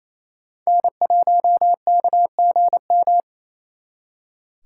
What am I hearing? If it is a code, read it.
Morse code, heard as a keyed tone, about 36 wpm, 700 Hz. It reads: N1KGM